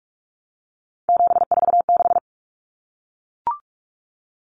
Morse code, heard as a keyed tone, about 44 wpm, 700 Hz. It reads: 746
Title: Common Cut Numbers - 3-Digit 45wpm